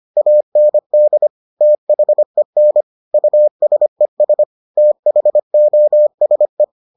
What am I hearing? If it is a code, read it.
Morse code, heard as a keyed tone, about 25 wpm, 600 Hz. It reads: AND THEN USES THOSE